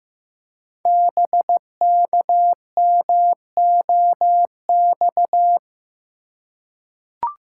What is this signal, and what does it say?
Morse code, heard as a keyed tone, about 15 wpm, 700 Hz. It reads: BKMOX
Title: Code Groups - 5 Letters - 15wpm